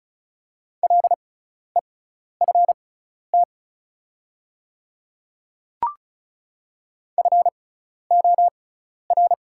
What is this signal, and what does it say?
Morse code, heard as a keyed tone, about 34 wpm, spaced as for 12 wpm, 690 Hz. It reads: LEFT FOR
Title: 6x Character Spacing - Top 300 Words - 35wpm